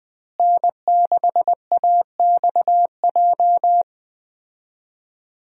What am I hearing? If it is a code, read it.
Morse code, heard as a keyed tone, about 20 wpm, 700 Hz. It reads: N6AXJ
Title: USA Call Signs - 5 Characters - Repeated 2x - 20wpm